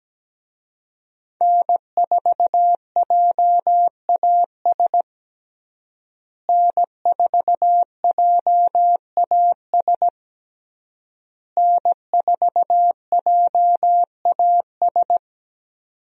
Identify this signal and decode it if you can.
Morse code, heard as a keyed tone, about 17 wpm, 700 Hz. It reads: N4JAS N4JAS N4JAS
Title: USA Call Signs - 5 Characters - Repeated 3x - 17wpm